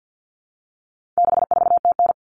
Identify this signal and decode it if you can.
Morse code, heard as a keyed tone, about 50 wpm, 700 Hz. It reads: 64TN
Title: All Cut Numbers - 4-Digit 50wpm